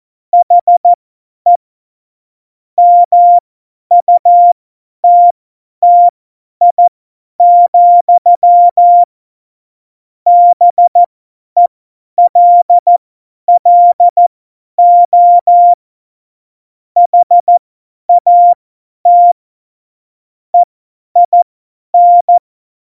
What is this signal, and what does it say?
Morse code, heard as a keyed tone, about 14 wpm, spaced as for 10 wpm, 700 Hz. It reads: HE MUTTI, BELLO HAT EIN
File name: Morsekurs / Witze / Witz 01. August 17, Tempo 50 F70